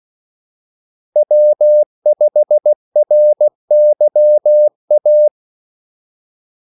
Morse code, heard as a keyed tone, about 16 wpm, 600 Hz. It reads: W5RYA
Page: Kaldesignaler 16 wpm